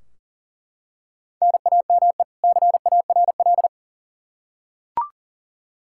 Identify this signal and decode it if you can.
Morse code, heard as a keyed tone, about 40 wpm, 700 Hz. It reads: NAME CARL
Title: Practice QSO - Exchange 1 - Beginning - 40wpm